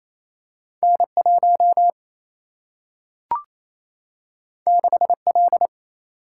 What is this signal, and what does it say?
Morse code, heard as a keyed tone, about 28 wpm, 700 Hz. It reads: N1 6L
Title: Single-Letter Single-Number 28wpm